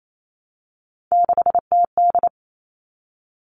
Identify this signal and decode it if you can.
Morse code, heard as a keyed tone, about 28 wpm, 700 Hz. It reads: 6TD
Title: Three Challenging Letter-Number Combinations 28wpm